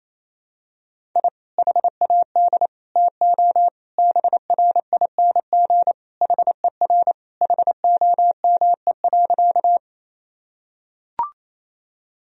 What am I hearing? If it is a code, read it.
Morse code, heard as a keyed tone, about 28 wpm, 700 Hz. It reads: I HAD TO BRING HER HOME.
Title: Sentences from Top 1000 Words - Review Part 3 - 28wpm